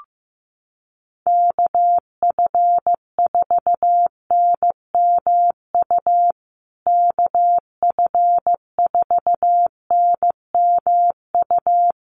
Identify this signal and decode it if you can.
Morse code, heard as a keyed tone, about 15 wpm, 700 Hz. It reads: KF4NMU KF4NMU
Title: USA Call Signs Repeated 3x 15wpm